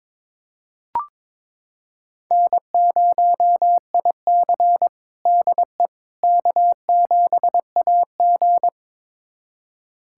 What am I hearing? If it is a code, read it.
Morse code, heard as a keyed tone, about 22 wpm, 700 Hz. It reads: N0IC DE K7AG
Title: USA Callsign DE Callsign - 4 Characters - 22wpm